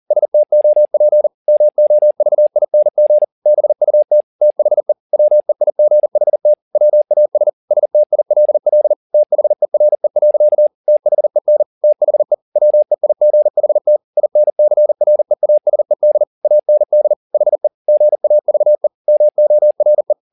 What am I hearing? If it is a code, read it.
Morse code, heard as a keyed tone, about 40 wpm, 600 Hz. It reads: STOP MOVING BUT THE WEIGHT WAS STILL THERE. THEN THE WEIGHT INCREASED AND HE GAVE MORE